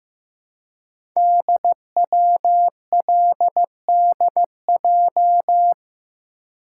Morse code, heard as a keyed tone, about 15 wpm, 700 Hz. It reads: DWLDJ
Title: Code Groups - 5 Letters - 15wpm